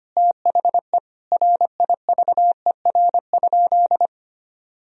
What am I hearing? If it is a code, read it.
Morse code, heard as a keyed tone, about 25 wpm, 700 Hz. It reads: THE RIVER?
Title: Sentences from Top 1000 Words - Review Part 1 - 25wpm